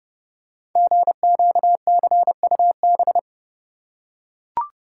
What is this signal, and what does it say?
Morse code, heard as a keyed tone, about 30 wpm, 700 Hz. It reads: GQCUB
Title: Code Groups - 5 Letter:Numbers - 30wpm